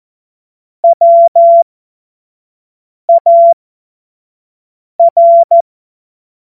Morse code, heard as a keyed tone, about 14 wpm, spaced as for 5 wpm, 680 Hz. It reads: WAR
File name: Morsekurs / Witze / Witz 02. August 2022 25 F70